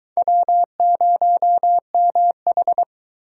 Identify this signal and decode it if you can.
Morse code, heard as a keyed tone, about 24 wpm, 700 Hz. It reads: W0MH